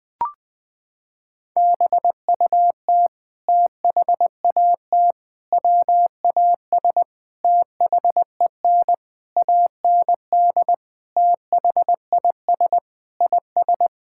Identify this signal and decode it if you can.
Morse code, heard as a keyed tone, about 20 wpm, 700 Hz. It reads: BUT THAT WAS THEN AND THIS IS
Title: Sentences with 8-Word Phrases - 20wpm